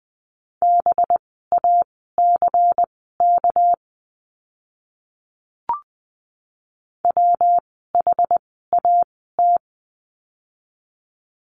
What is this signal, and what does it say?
Morse code, heard as a keyed tone, about 20 wpm, spaced as for 14 wpm, 700 Hz. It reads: BACK WHAT
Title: 199 Lesson 32f - Letter k - Words 20wpm